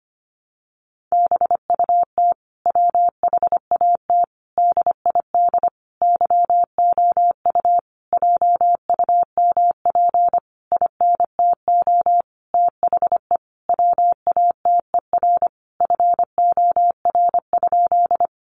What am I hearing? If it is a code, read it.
Morse code, heard as a keyed tone, about 25 wpm, 700 Hz. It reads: BUT WHAT DID YOU JUMP INTO THE WATER FOR?
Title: Sentences from Top 1300 Words 25wpm